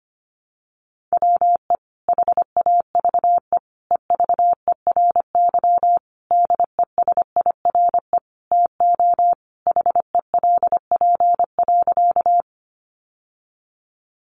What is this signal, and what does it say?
Morse code, heard as a keyed tone, about 25 wpm, 700 Hz. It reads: WE HAVE EVERY DESIRE TO HELP.
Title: Sentences from Top 1100 Words 25wpm